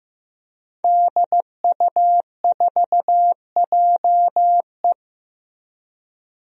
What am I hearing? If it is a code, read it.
Morse code, heard as a keyed tone, about 16 wpm, 700 Hz. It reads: DU4JE
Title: DXCC-100 Call Signs - 5 Characters - Repeated 3x - 15wpm